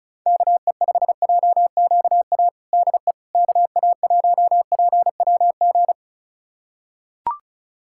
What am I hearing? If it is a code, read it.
Morse code, heard as a keyed tone, about 36 wpm, 700 Hz. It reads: KE5JQA DE KA1PWG
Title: USA Callsign DE Callsign 35wpm